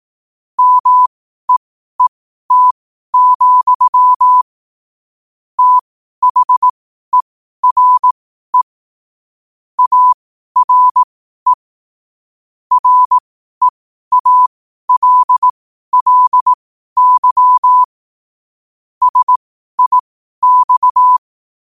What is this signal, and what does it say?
Morse code, heard as a keyed tone, about 18 wpm, spaced as for 12 wpm, 1000 Hz. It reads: MEET, THERE ARE REALLY SIX
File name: qotdfiles / 12wpm